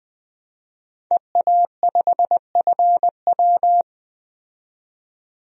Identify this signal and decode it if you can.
Morse code, heard as a keyed tone, about 20 wpm, 700 Hz. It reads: EA5FW
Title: DXCC-100 Call Signs - 5 Characters - 20wpm